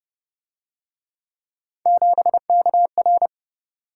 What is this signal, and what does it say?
Morse code, heard as a keyed tone, about 29 wpm, 700 Hz. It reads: 7KR
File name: Speed-Racing - Three Letter-Number Combinations - 25wpm